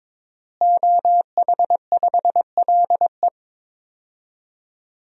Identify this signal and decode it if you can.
Morse code, heard as a keyed tone, about 22 wpm, 700 Hz. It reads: OH5LE
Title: DXCC-100 Call Signs - 5 Characters - Repeated 2x - 22wpm